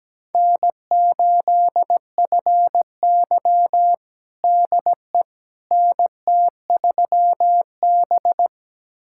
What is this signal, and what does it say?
Morse code, heard as a keyed tone, about 17 wpm, 700 Hz. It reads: N8FY DE NT3B
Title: USA Callsign DE Callsign - 4 Characters - 17wpm